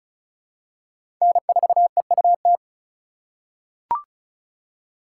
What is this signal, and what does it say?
Morse code, heard as a keyed tone, about 34 wpm, 700 Hz. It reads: N4EUT
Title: USA Call Signs - 5 Characters - 4x Character Spacing - 35wpm